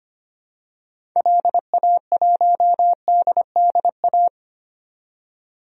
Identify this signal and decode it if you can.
Morse code, heard as a keyed tone, about 25 wpm, 700 Hz. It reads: LA1DDA
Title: DXCC-100 Call Signs - 6 Characters - 25wpm